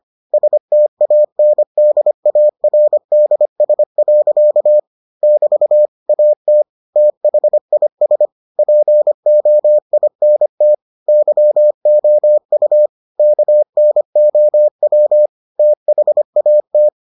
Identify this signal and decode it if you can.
Morse code, heard as a keyed tone, about 25 wpm, 600 Hz. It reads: STANDARDS. <BT> AT THIS POINT YOU KNOW THAT